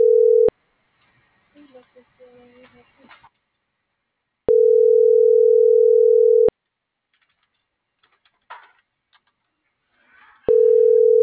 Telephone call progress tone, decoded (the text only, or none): ringback tone